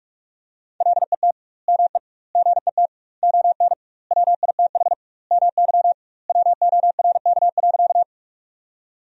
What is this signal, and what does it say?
Morse code, heard as a keyed tone, about 43 wpm, 700 Hz. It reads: LET ME GET ON WITH MY WORK.
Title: 2x Word-Spacing - Sentences from Top 300 Words - 45wpm